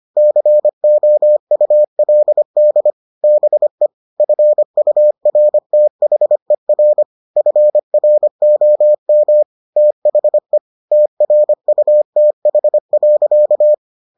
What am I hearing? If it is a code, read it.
Morse code, heard as a keyed tone, about 25 wpm, 600 Hz. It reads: COULD BE FURTHER FROM THE TRUTH.